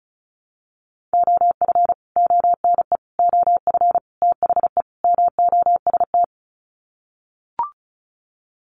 Morse code, heard as a keyed tone, about 34 wpm, 700 Hz. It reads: OF ONE OF THE MOST
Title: Sets of 5 Words Encore2 35wpm